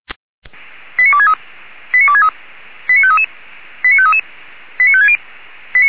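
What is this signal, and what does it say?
Five-tone selective call (ZVEI), heard as quick runs of five tones, tones of about 70 ms. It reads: 89263 89263 89533 89533 89560
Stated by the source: ZVEI five-tone call 89248, received by SDR.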